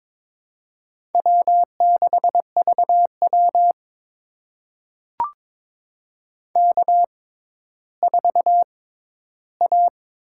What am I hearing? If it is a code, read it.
Morse code, heard as a keyed tone, about 22 wpm, spaced as for 13 wpm, 700 Hz. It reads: W6VW K4A
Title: USA Call Signs - 6x Character Spacing - 22wpm